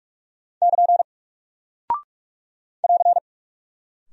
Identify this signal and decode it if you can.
Morse code, heard as a keyed tone, about 46 wpm, 700 Hz. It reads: <KN> <AR>
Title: Single Character (Letter:Number:Punctuation:Prosign) Alt 45wpm